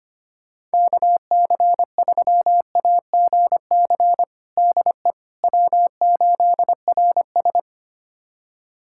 Morse code, heard as a keyed tone, about 25 wpm, 700 Hz. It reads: KC3AGC DE W8RS
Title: USA Callsign DE Callsign 25wpm